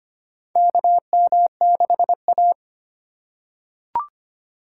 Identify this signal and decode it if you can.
Morse code, heard as a keyed tone, about 25 wpm, 700 Hz. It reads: KM6A
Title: USA Call Signs - 4 Characters - 4x Character Spacing - 25wpm